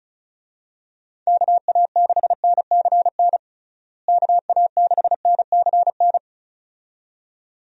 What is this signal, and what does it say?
Morse code, heard as a keyed tone, about 36 wpm, 700 Hz. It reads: KA6NCN KA6NCN
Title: USA Call Signs - 6 Characters - Repeated 2x - 35wpm